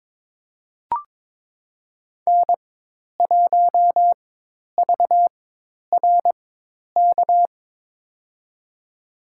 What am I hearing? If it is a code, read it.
Morse code, heard as a keyed tone, about 22 wpm, spaced as for 10 wpm, 700 Hz. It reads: N1VRK
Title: USA Call Signs - 4x Character Spacing - 22wpm